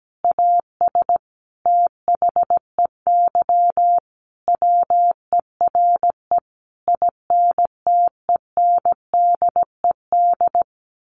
Morse code, heard as a keyed tone, about 17 wpm, 700 Hz. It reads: AS THEY WERE INTENDED
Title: A Sound Of Thunder 17wpm